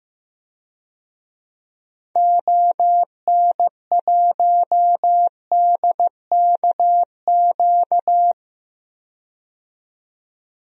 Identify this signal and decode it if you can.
Morse code, heard as a keyed tone, about 15 wpm, 700 Hz. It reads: ON1DKQ
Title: DXCC-100 Call Signs - 6 Characters - 15wpm